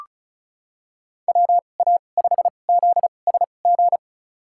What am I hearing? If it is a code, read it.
Morse code, heard as a keyed tone, about 36 wpm, spaced as for 25 wpm, 690 Hz. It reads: WA5ZSG
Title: USA Call Signs - 6 Characters - 2x Character Spacing - 35wpm